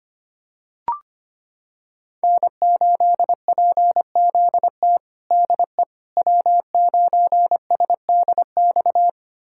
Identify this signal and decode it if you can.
Morse code, heard as a keyed tone, about 25 wpm, 700 Hz. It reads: N8PZT DE W9SDX
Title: USA Callsign DE Callsign - 5 Characters - 25wpm